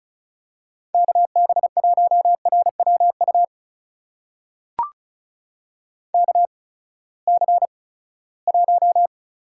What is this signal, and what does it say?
Morse code, heard as a keyed tone, about 36 wpm, spaced as for 20 wpm, 700 Hz. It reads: KB1RWU KC1
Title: USA Call Signs - 6 Characters - 8x Character Spacing - 35wpm